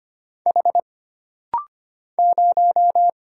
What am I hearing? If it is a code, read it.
Morse code, heard as a keyed tone, about 25 wpm, 700 Hz. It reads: H 0
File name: Single Character (Letter:Number:Punctuation:Prosign) - Rapid-Fire - 25wpm